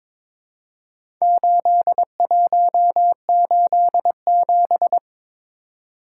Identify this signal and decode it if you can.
Morse code, heard as a keyed tone, about 22 wpm, 700 Hz. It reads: 8187
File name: practice / 22wpm / Four Letter-Number Combinations - 22wpm